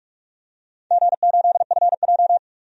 Morse code, heard as a keyed tone, about 43 wpm, 700 Hz. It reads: G8FJ
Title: DXCC-100 Call Signs - 4 Characters - 45wpm